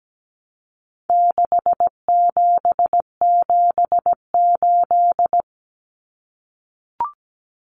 Morse code, heard as a keyed tone, about 17 wpm, 700 Hz. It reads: 6778
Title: Numbers - 4-Digit 17wpm